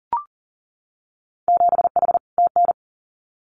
Morse code, heard as a keyed tone, about 40 wpm, 700 Hz. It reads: 7H TN
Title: Winter Field Day Exchanges Encore 40wpm